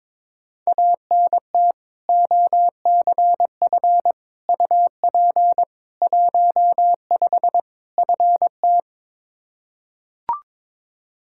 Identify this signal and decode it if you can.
Morse code, heard as a keyed tone, about 22 wpm, 700 Hz. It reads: ANT OCF UP 15 FT
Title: Practice QSO - Exchange 2 - Info - 22wpm